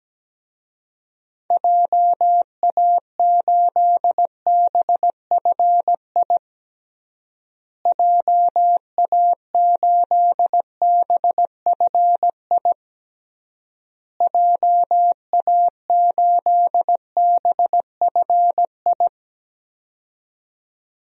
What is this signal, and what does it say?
Morse code, heard as a keyed tone, about 17 wpm, 700 Hz. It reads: JA8BFI JA8BFI JA8BFI
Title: DXCC-100 Call Signs - 6 Characters - Repeated 3x - 17wpm